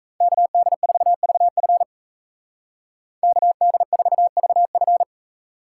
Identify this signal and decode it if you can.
Morse code, heard as a keyed tone, about 34 wpm, 700 Hz. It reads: KD4VF KD4VF
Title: Speed Racing - USA 2x2 Call Signs 28wpm